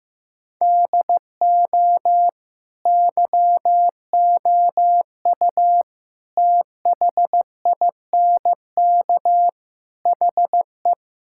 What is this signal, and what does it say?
Morse code, heard as a keyed tone, about 15 wpm, 700 Hz. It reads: DO YOU THINK HE
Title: Sentences from Top 1600 Words 15wpm